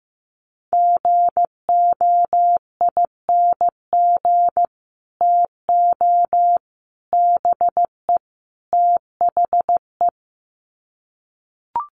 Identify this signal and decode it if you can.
Morse code, heard as a keyed tone, about 15 wpm, 700 Hz. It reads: GOING TO BE THE
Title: Sets of 4 Words 15wpm